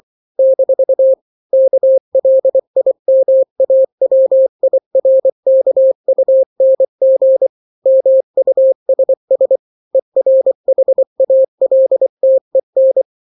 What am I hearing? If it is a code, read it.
Morse code, heard as a keyed tone, about 24 wpm, 530 Hz. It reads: - KLIMAWIRKUNG MUSS ERHALTEN
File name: daily / 23wpm